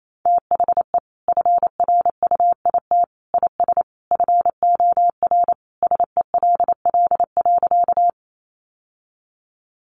Morse code, heard as a keyed tone, about 28 wpm, 700 Hz. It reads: THE FRUIT IS FOR SELL.
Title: Sentences from Top 1300 Words 28wpm